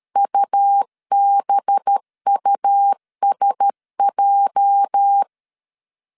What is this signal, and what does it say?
Morse code, heard as a keyed tone, about 13 wpm, 800 Hz. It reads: UBUSJ
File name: CW / DET / 1 9 1500C S B 12wpm 3819